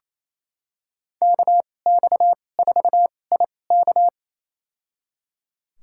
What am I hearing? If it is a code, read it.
Morse code, heard as a keyed tone, about 28 wpm, spaced as for 20 wpm, 700 Hz. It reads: KX4IK
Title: USA Call Signs - 5 Characters - 2x Character Spacing - 28wpm